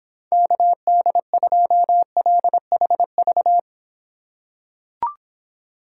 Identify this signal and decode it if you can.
Morse code, heard as a keyed tone, about 26 wpm, 700 Hz. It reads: KD2LHV